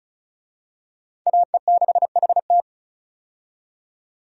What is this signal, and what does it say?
Morse code, heard as a keyed tone, about 34 wpm, 700 Hz. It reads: AE6HT
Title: USA Call Signs - 5 Characters - 35wpm